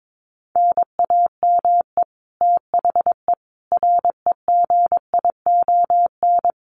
Morse code, heard as a keyed tone, about 22 wpm, 700 Hz. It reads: NAME THE REGION
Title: Sentences from Top 1500 Words 22wpm